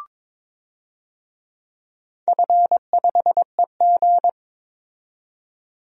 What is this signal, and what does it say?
Morse code, heard as a keyed tone, about 22 wpm, 700 Hz. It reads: F5EG
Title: DXCC-100 Call Signs - 4 Characters - 22wpm